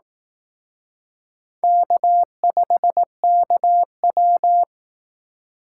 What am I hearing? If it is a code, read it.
Morse code, heard as a keyed tone, about 18 wpm, 700 Hz. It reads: K5KW
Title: Speed Racing - USA 1x2 Call Signs 15wpm